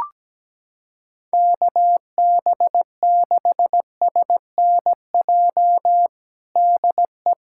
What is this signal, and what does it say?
Morse code, heard as a keyed tone, about 17 wpm, 700 Hz. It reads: KB6SNJ DE
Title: USA Callsign DE Callsign - 6 Characters - 17wpm